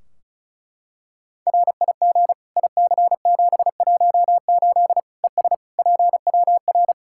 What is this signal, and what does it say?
Morse code, heard as a keyed tone, about 36 wpm, 700 Hz. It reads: RIG IC718 ES PWR
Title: Practice QSO - Exchange 2 - Info - 35wpm